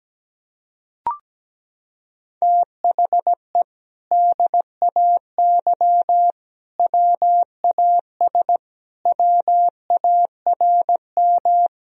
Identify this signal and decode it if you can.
Morse code, heard as a keyed tone, about 17 wpm, 700 Hz. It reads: THE DAY WAS WARM
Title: Sentences from Top 1000 Words - Review Part 2 - 17wpm